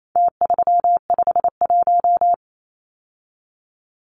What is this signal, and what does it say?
Morse code, heard as a keyed tone, about 28 wpm, 700 Hz. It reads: T351
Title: Numbers - Contest Style - 4-Digit 28wpm